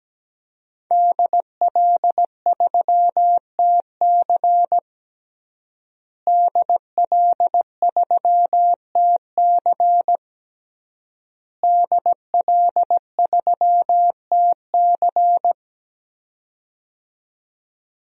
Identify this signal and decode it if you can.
Morse code, heard as a keyed tone, about 17 wpm, 700 Hz. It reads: DL3TC DL3TC DL3TC
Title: DXCC-100 Call Signs - 5 Characters - Repeated 3x - 17wpm